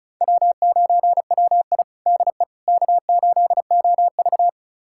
Morse code, heard as a keyed tone, about 36 wpm, 700 Hz. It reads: W9WI DE K8OV
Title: USA Callsign DE Callsign - 4 Characters - 35wpm